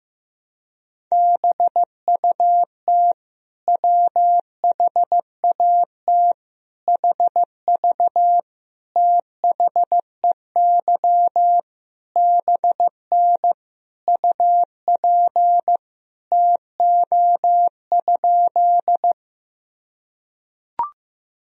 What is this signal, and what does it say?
Morse code, heard as a keyed tone, about 15 wpm, 700 Hz. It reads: BUT WHAT HV THEY BN UP TO?
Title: Sentences from Top 100 Words and Abbreviations 15wpm